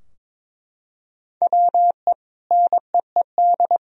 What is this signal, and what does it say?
Morse code, heard as a keyed tone, about 22 wpm, 700 Hz. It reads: WE NEED
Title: Sentences from Top 1000 Words - Review Part 4 - 22wpm